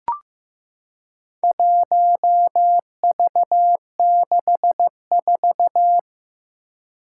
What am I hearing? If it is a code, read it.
Morse code, heard as a keyed tone, about 15 wpm, 700 Hz. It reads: 1V64
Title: Four Challenging Letter-Number Combinations 15wpm